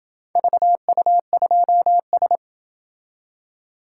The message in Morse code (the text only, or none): VU2S